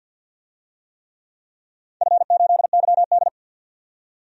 Morse code, heard as a keyed tone, about 51 wpm, 700 Hz. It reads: F8YD
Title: DXCC-100 Call Signs - 4 Characters - 50wpm